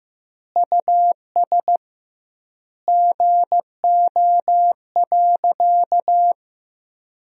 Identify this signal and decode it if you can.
Morse code, heard as a keyed tone, about 15 wpm, 700 Hz. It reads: US GO.
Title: 2x Word-Spacing - Sentences from Top 400 Words - 15wpm